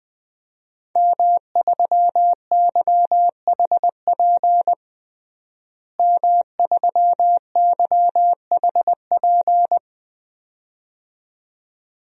Morse code, heard as a keyed tone, about 20 wpm, 700 Hz. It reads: M3YHP M3YHP
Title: DXCC-100 Call Signs - 5 Characters - Repeated 2x - 20wpm